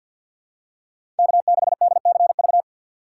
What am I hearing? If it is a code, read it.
Morse code, heard as a keyed tone, about 48 wpm, 700 Hz. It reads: K6DCV
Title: USA Call Signs - 4x Character Spacing - 50wpm